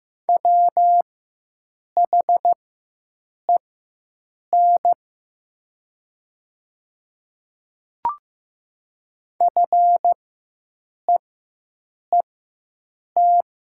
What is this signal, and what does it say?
Morse code, heard as a keyed tone, about 15 wpm, spaced as for 7 wpm, 700 Hz. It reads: WHEN FEET
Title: 4x Character Spacing - Top 300 Words - 15wpm